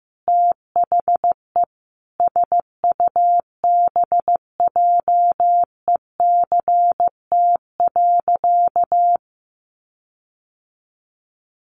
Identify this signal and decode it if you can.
Morse code, heard as a keyed tone, about 15 wpm, 700 Hz. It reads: THE SUBJECT.
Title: Sentences from Top 1400 Words 15wpm